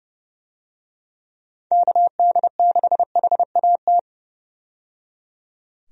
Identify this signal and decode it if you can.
Morse code, heard as a keyed tone, about 30 wpm, 700 Hz. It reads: KD6HAT
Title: USA Call Signs - 6 Characters - 30wpm